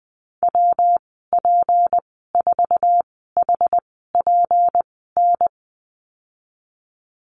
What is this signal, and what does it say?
Morse code, heard as a keyed tone, about 20 wpm, spaced as for 15 wpm, 700 Hz. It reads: WP4HPN
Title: 167 Lesson 27g - Number 7 - Call Signs 20wpm